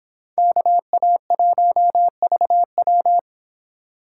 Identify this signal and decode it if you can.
Morse code, heard as a keyed tone, about 31 wpm, 700 Hz. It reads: KA1VW